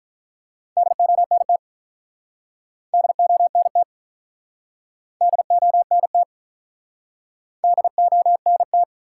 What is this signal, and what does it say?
Morse code, heard as a keyed tone, about 48 wpm, 700 Hz. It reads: DONT DONT DONT DONT